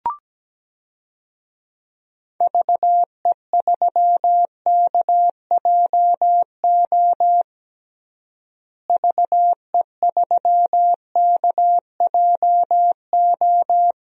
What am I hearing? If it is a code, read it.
Morse code, heard as a keyed tone, about 17 wpm, 700 Hz. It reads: VE3KJO VE3KJO
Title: DXCC-100 Call Signs - 6 Characters - Repeated 2x - 17wpm